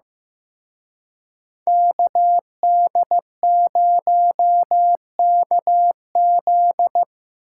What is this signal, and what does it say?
Morse code, heard as a keyed tone, about 17 wpm, 700 Hz. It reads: KD0KZ
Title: Speed Racing - USA 2x2 Call Signs 15wpm